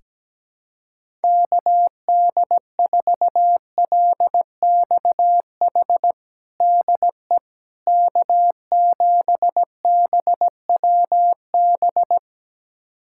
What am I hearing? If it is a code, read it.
Morse code, heard as a keyed tone, about 17 wpm, 700 Hz. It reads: KD4LXH DE K7BWB
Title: USA Callsign DE Callsign 17wpm